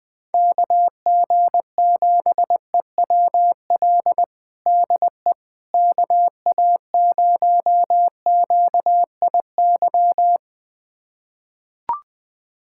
Morse code, heard as a keyed tone, about 21 wpm, 700 Hz. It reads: KG7EWL DE KA0QIY
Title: USA Callsign DE Callsign - 6 Characters - 20wpm